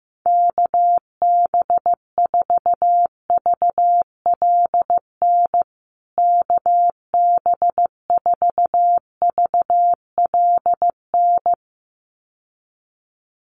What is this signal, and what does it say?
Morse code, heard as a keyed tone, about 15 wpm, 700 Hz. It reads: KB4VLN KB4VLN
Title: USA Call Signs Repeated 2x 15wpm